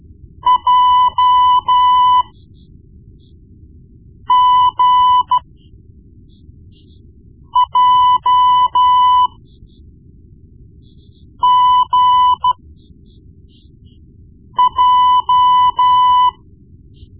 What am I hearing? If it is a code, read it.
Morse code, heard as a keyed tone, about 12 wpm, spaced as for 4 wpm, 1000 Hz. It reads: JGJGJ
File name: Morse Lessons / Morse Lessons / Tape-1B